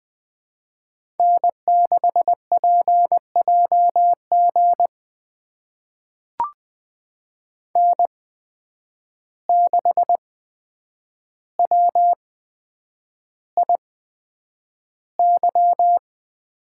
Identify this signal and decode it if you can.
Morse code, heard as a keyed tone, about 20 wpm, spaced as for 9 wpm, 700 Hz. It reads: N6PJG N6WIY
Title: USA Call Signs - 5 Characters - 8x Character Spacing - 20wpm